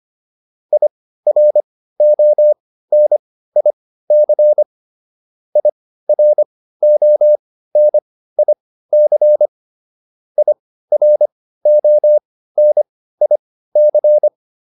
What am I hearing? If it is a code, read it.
Morse code, heard as a keyed tone, about 25 wpm, spaced as for 15 wpm, 600 Hz. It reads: IRONIC IRONIC IRONIC